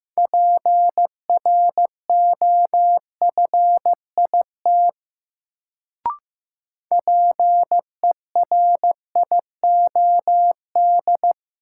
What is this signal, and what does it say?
Morse code, heard as a keyed tone, about 15 wpm, 700 Hz. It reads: PROFIT PERIOD
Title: Sentences from Top 800 Words - New Word Drill - 15wpm